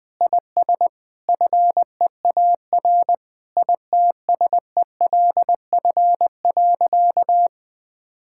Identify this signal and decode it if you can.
Morse code, heard as a keyed tone, about 20 wpm, 700 Hz. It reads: IS FEAR ITSELF.
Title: Sentences with 8-Word Phrases - Spoken Beginning - 20wpm